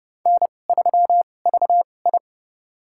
Morse code, heard as a keyed tone, about 29 wpm, spaced as for 21 wpm, 700 Hz. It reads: N3VI
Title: USA Call Signs - 4 Characters - 2x Character Spacing - 30wpm